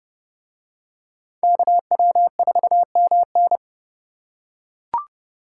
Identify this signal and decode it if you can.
Morse code, heard as a keyed tone, about 29 wpm, 700 Hz. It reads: KW4MN